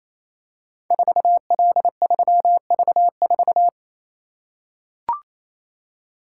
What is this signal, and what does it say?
Morse code, heard as a keyed tone, about 28 wpm, 700 Hz. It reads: 4L3V4
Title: Code Groups - 5 Letter:Numbers - 28wpm